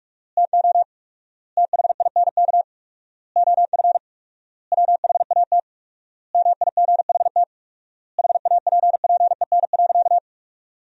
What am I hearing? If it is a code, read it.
Morse code, heard as a keyed tone, about 46 wpm, 700 Hz. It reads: TO THINK OF WHAT MIGHT HAPPEN.
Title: 4x Word-Spacing - Sentences from Top 400 Words - 45wpm